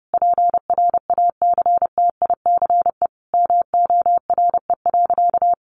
Morse code, heard as a keyed tone, about 30 wpm, 700 Hz. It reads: PRACTICE MORE.
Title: Sentences from Top 700 Words 30wpm